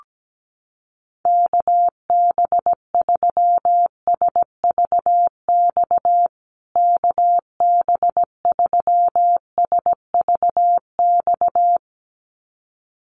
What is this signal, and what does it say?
Morse code, heard as a keyed tone, about 17 wpm, 700 Hz. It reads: KB3SVX KB3SVX
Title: USA Call Signs Repeated 2x 17wpm